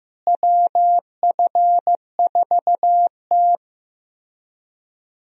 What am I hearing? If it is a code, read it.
Morse code, heard as a keyed tone, about 15 wpm, 700 Hz. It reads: WF4T
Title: USA Call Signs - 4 Characters - Repeated 3x - 15wpm